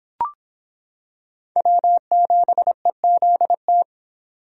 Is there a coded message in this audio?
Yes: Morse code from 1.6 to 3.8 s, heard as a keyed tone, about 26 wpm, 700 Hz. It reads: W7EZT